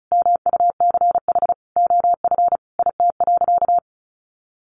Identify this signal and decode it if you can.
Morse code, heard as a keyed tone, about 34 wpm, 700 Hz. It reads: MUCH OF IT.
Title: Sentences from Top 1400 Words 35wpm